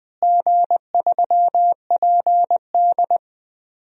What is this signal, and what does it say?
Morse code, heard as a keyed tone, about 20 wpm, 700 Hz. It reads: G3PD
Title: DXCC-100 Call Signs - 4 Characters - Repeated 2x - 20wpm